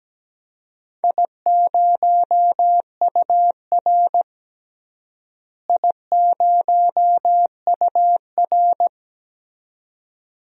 Morse code, heard as a keyed tone, about 17 wpm, 700 Hz. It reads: I0UR I0UR
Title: DXCC-100 Call Signs - 4 Characters - Repeated 2x - 17wpm